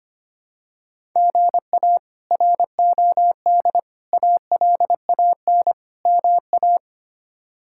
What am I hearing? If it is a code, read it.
Morse code, heard as a keyed tone, about 25 wpm, 700 Hz. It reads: GA ROD ALAN MA
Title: K1USN SST QSO Practice - 25wpm